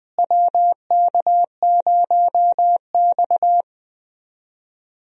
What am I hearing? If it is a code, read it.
Morse code, heard as a keyed tone, about 20 wpm, 700 Hz. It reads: WK0X
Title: USA Call Signs - 4 Characters - Repeated 2x - 20wpm